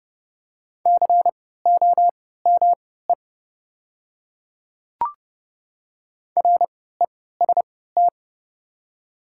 Morse code, heard as a keyed tone, about 30 wpm, spaced as for 17 wpm, 700 Hz. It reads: COME REST
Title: 3x Character Spacing - Top 500 Words - 30wpm